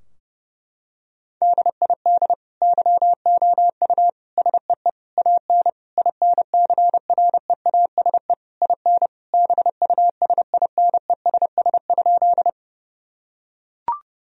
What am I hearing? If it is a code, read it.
Morse code, heard as a keyed tone, about 30 wpm, 700 Hz. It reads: DID YOU SEE AN INCREASE IN BUSINESS?
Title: Sentences from Top 1000 Words - Review Part 4 - 30wpm